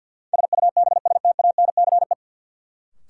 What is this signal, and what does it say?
Morse code, heard as a keyed tone, about 53 wpm, 700 Hz. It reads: SUBSTANCE
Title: Sentences from Top 1500 Words - New Word Drill Alt - 50wpm